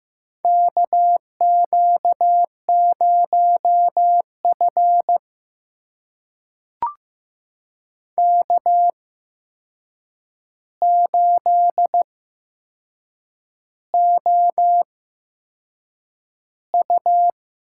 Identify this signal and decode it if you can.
Morse code, heard as a keyed tone, about 15 wpm, spaced as for 6 wpm, 700 Hz. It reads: KQ0F K8OU
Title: USA Call Signs - 4 Characters - 8x Character Spacing - 15wpm